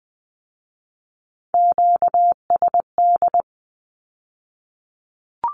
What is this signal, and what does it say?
Morse code, heard as a keyed tone, about 20 wpm, 700 Hz. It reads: QSD
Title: Q-Codes with Spoken Meaning - 20wpm